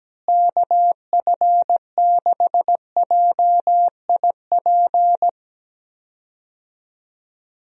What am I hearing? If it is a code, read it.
Morse code, heard as a keyed tone, about 17 wpm, 700 Hz. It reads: KF6JIP
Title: USA Call Signs - 6 Characters - Repeated 3x - 17wpm